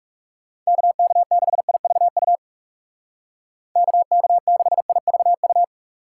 Morse code, heard as a keyed tone, about 35 wpm, 700 Hz. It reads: KK6IVU KK6IVU
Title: Speed Racing - USA 2x3 Call Signs 30wpm